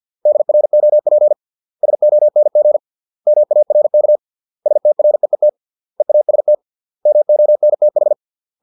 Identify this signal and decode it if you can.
Morse code, heard as a keyed tone, about 50 wpm, 600 Hz. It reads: DROP SONG MARK STREET EAST MONTH